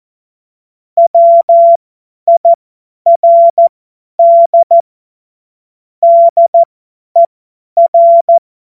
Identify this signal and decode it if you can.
Morse code, heard as a keyed tone, about 14 wpm, spaced as for 10 wpm, 680 Hz. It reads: WIRD DER
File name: Morsekurs / Witze / Witz 27. Januar 2026 50 F70